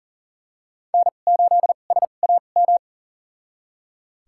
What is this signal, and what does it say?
Morse code, heard as a keyed tone, about 40 wpm, spaced as for 29 wpm, 700 Hz. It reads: N8SAM
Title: USA Call Signs - 5 Characters - 2x Character Spacing - 40wpm